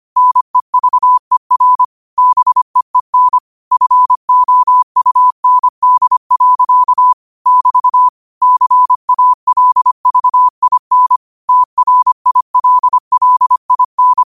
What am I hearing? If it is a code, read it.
Morse code, heard as a keyed tone, about 25 wpm, 1000 Hz. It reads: NEVER BEEN FOUND. <BT> CALVIN TRILLIN